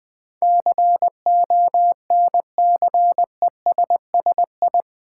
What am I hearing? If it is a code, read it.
Morse code, heard as a keyed tone, about 20 wpm, 700 Hz. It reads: CONCESSI
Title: Build A Word - 4 Letters At A Time - 10-Letter Words - 20wpm